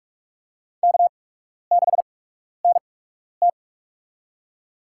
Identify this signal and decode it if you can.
Morse code, heard as a keyed tone, about 43 wpm, spaced as for 12 wpm, 700 Hz. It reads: K6NT
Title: USA Call Signs - 4 Characters - 8x Character Spacing - 45wpm